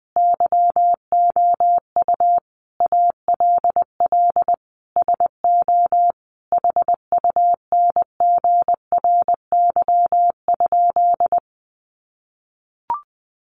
Morse code, heard as a keyed tone, about 20 wpm, 700 Hz. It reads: YOU ALL SO HUNGRY?
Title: Sentences from Top 1400 Words 20wpm